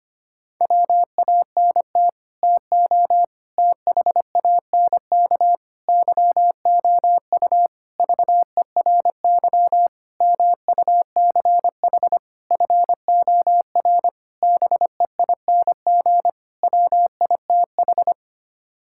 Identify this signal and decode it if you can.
Morse code, heard as a keyed tone, about 25 wpm, 700 Hz. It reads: WANT TO THANK YOU VERY MUCH FOR BEING WITH
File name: Sets of 9 Words - 25wpm